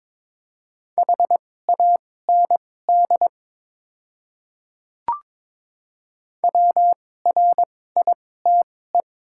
Morse code, heard as a keyed tone, about 22 wpm, spaced as for 16 wpm, 700 Hz. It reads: HAND WRITE
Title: 2x Character Spacing - Top 500 Words - 22wpm